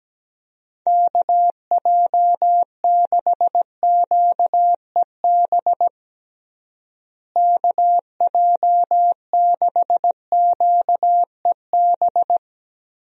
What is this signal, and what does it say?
Morse code, heard as a keyed tone, about 17 wpm, 700 Hz. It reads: KJ6QEB KJ6QEB
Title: USA Call Signs - 6 Characters - Repeated 2x - 17wpm